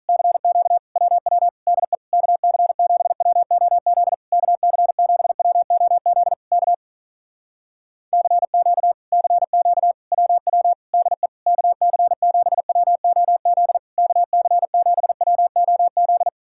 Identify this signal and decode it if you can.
Morse code, heard as a keyed tone, about 48 wpm, 700 Hz. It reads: CQ WW DE KC7WOZ KC7WOZ K CQ CQ WW DE KC7WOZ KC7WOZ